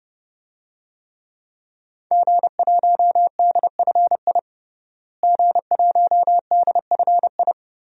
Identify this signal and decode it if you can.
Morse code, heard as a keyed tone, about 32 wpm, 700 Hz. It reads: G1DFI G1DFI
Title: DXCC-100 Call Signs - 5 Characters - Repeated 3x - 30wpm